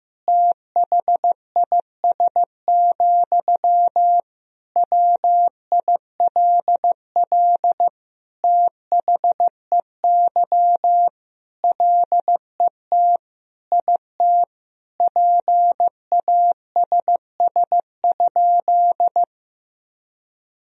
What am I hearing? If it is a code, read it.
Morse code, heard as a keyed tone, about 15 wpm, 700 Hz. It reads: THIS, WILL THEY LET IT PASS?
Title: Sentences from Top 400 Words and Abbreviations 15wpm